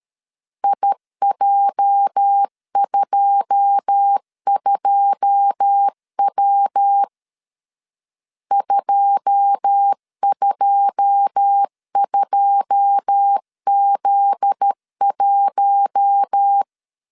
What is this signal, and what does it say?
Morse code, heard as a keyed tone, about 13 wpm, 800 Hz. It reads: IJ22W 222Z1
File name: LEZIONI / 9 2 12wpm 2301